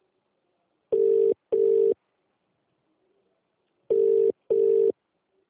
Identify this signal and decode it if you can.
Ringback tone (UK), 400 + 450 Hz.